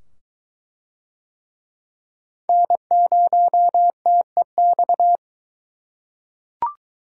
Morse code, heard as a keyed tone, about 20 wpm, 700 Hz. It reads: N0TEX